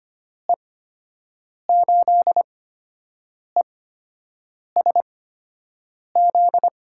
Morse code, heard as a keyed tone, about 25 wpm, spaced as for 7 wpm, 700 Hz. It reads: E8ESZ
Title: USA Call Signs - 6 Characters - 8x Character Spacing - 25wpm